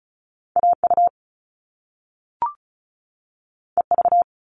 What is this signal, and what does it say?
Morse code, heard as a keyed tone, about 36 wpm, 700 Hz. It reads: AU EV
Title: Top Two Letter Combinations in Words 35wpm